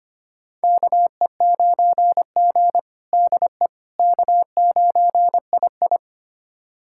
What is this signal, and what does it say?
Morse code, heard as a keyed tone, about 25 wpm, 700 Hz. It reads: KE9G DE K9II
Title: USA Callsign DE Callsign - 4 Characters - 25wpm